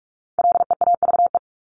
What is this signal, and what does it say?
Morse code, heard as a keyed tone, about 44 wpm, 700 Hz. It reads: LEAVE
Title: Top 500 Words Repeated 45wpm